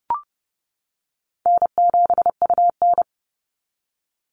Morse code, heard as a keyed tone, about 30 wpm, 700 Hz. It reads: N7UN
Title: NA SOTA Activator Callsigns 30wpm